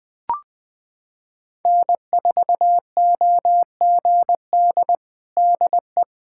Morse code, heard as a keyed tone, about 19 wpm, 700 Hz. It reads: N4OGD DE
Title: Practice QSO - Exchange 4 - Ending - 20wpm